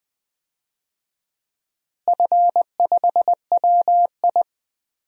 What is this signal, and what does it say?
Morse code, heard as a keyed tone, about 20 wpm, 700 Hz. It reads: F5WI
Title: DXCC-100 Call Signs - 4 Characters - Repeated 3x - 20wpm